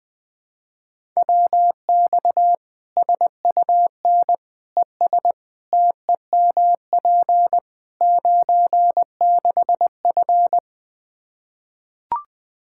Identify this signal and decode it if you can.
Morse code, heard as a keyed tone, about 20 wpm, 700 Hz. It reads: WX SUN ES TEMP 96F
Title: Practice QSO - Exchange 2 - Info - 20wpm